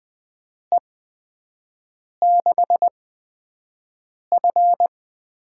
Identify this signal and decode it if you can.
Morse code, heard as a keyed tone, about 20 wpm, spaced as for 5 wpm, 700 Hz. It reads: E6F
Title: USA Call Signs - 6 Characters - 8x Character Spacing - 20wpm